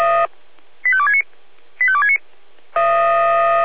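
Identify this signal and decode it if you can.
Five-tone selective call (ZVEI), heard as quick runs of five tones, tones of about 70 ms. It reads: 85269 85269
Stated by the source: ZVEI five-tone selective call received by SDR.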